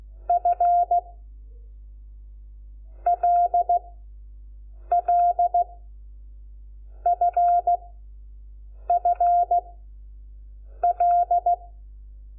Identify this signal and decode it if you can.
Morse code, heard as a keyed tone, about 16 wpm, 700 Hz. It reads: F L L F F L
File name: Lecciones / CLASE Nº5